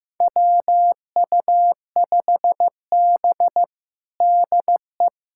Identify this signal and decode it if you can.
Morse code, heard as a keyed tone, about 15 wpm, 700 Hz. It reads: WU5B DE
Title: USA Callsign DE Callsign - 4 Characters - 15wpm